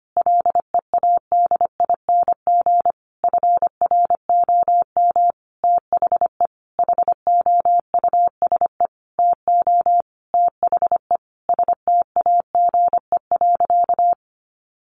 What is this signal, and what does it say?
Morse code, heard as a keyed tone, about 25 wpm, 700 Hz. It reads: LEADING FROM THE HOUSE TO THE STAGE.
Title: Sentences from Top 1200 Words 25wpm